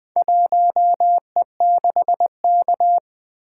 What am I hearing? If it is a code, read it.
Morse code, heard as a keyed tone, about 20 wpm, 700 Hz. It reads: 1E6K
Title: Four Letter-Number Combinations - 20wpm